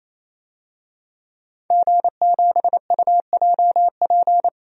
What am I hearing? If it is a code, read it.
Morse code, heard as a keyed tone, about 27 wpm, 700 Hz. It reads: G7UJP
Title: DXCC-100 Call Signs - 5 Characters - 28wpm